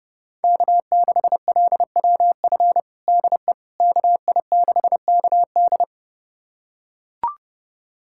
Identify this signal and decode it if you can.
Morse code, heard as a keyed tone, about 30 wpm, 700 Hz. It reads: K6LWF DE KI6KD
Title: USA Callsign DE Callsign - 5 Characters - 30wpm